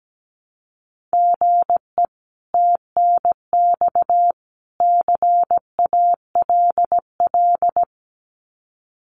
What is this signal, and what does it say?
Morse code, heard as a keyed tone, about 17 wpm, 700 Hz. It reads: GE TNX CALL
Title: Common QSO Phrases 17wpm